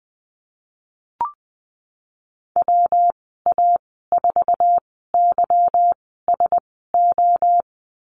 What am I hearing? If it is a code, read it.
Morse code, heard as a keyed tone, about 20 wpm, spaced as for 15 wpm, 700 Hz. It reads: WA4YSO
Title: 161 Lesson 26g - Letter g - Call Signs 20wpm